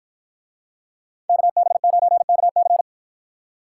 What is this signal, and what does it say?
Morse code, heard as a keyed tone, about 53 wpm, 700 Hz. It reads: KB9KC